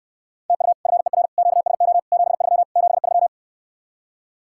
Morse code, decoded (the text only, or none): TU FER QSO 73 73